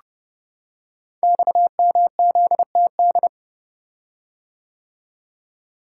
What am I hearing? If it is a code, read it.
Morse code, heard as a keyed tone, about 28 wpm, 700 Hz. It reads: XMZTD
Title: Code Groups - 5 Letter:Numbers - 30wpm